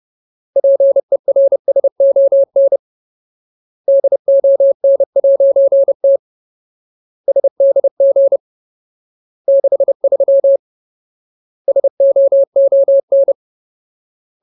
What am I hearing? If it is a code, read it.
Morse code, heard as a keyed tone, about 30 wpm, 550 Hz. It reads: PERSON DON'T SDG 63 SOON
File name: Ultimate Word List / 30WPM Ultimate